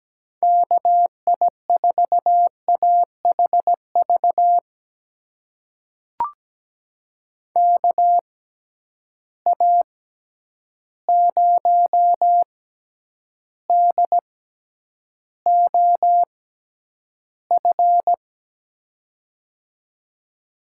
Morse code, heard as a keyed tone, about 17 wpm, spaced as for 9 wpm, 700 Hz. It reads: KI4AHV KA0DOF
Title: USA Call Signs - 6 Characters - 6x Character Spacing - 17wpm